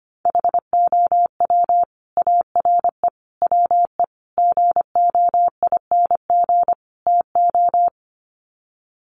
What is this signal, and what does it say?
Morse code, heard as a keyed tone, about 25 wpm, 700 Hz. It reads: HOW ARE WE GOING TO
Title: Sets of 5 Words Encore 25wpm